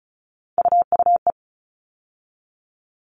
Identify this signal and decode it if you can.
Morse code, heard as a keyed tone, about 34 wpm, 700 Hz. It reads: UUE
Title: All Cut Numbers - 3-Digit 35wpm